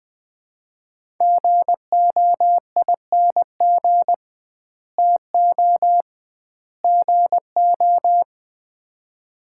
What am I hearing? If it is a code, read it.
Morse code, heard as a keyed tone, about 20 wpm, 700 Hz. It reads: GOING TO GO
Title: 2x Word-Spacing - Sets of 3 Words - 20wpm